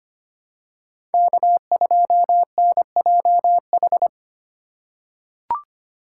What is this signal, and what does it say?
Morse code, heard as a keyed tone, about 25 wpm, 700 Hz. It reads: K2NJH
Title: USA Call Signs - 5 Characters - 4x Character Spacing - 25wpm